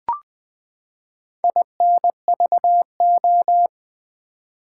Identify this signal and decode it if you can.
Morse code, heard as a keyed tone, about 20 wpm, 700 Hz. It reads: INVO
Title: Build A Word - 4 Letters At A Time - 8-Letter Words - 20wpm